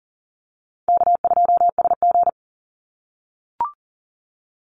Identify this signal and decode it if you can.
Morse code, heard as a keyed tone, about 41 wpm, 700 Hz. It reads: K2SG
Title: Easy Call Signs - 40wpm